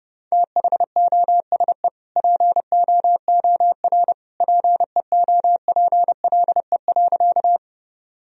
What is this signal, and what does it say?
Morse code, heard as a keyed tone, about 30 wpm, 700 Hz. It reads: THOSE POOR PEOPLE.
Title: Sentences from Top 1000 Words - Review Part 1 - 30wpm